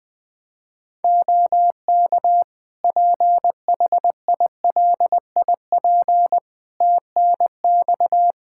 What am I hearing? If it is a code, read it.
Morse code, heard as a keyed tone, about 21 wpm, 700 Hz. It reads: OK PHILIP TNX
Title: K1USN SST QSO Practice 20wpm